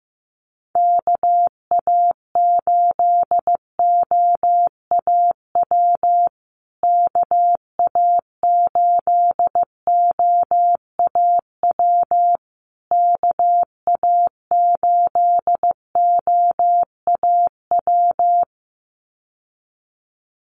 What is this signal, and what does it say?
Morse code, heard as a keyed tone, about 15 wpm, 700 Hz. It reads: KA8OAW KA8OAW KA8OAW
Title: USA Call Signs Repeated 3x 15wpm